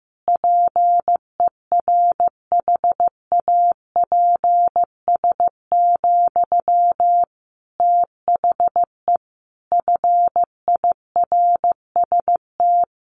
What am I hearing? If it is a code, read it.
Morse code, heard as a keyed tone, about 15 wpm, 700 Hz. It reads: PERHAPS, THE FIRST
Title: Sentences from Top 1500 Words 15wpm